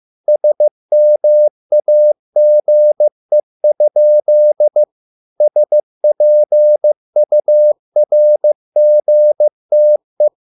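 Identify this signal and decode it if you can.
Morse code, heard as a keyed tone, about 15 wpm, 600 Hz. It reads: SMAGE? SPURGTE
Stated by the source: Morse code at a slow 15 wpm from CW med Gnister.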